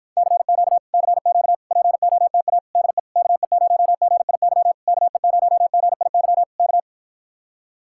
Morse code, heard as a keyed tone, about 53 wpm, 700 Hz. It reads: CQ CQ POTA DE KE0GIY KE0GIY K